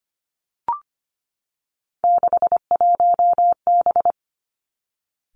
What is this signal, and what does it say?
Morse code, heard as a keyed tone, about 25 wpm, 700 Hz. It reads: 61B
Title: Three Challenging Letter-Number Combinations 25wpm